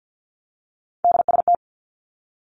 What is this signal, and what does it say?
Morse code, heard as a keyed tone, about 50 wpm, 700 Hz. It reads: DST
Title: Three Challenging Letter-Number Combinations 50wpm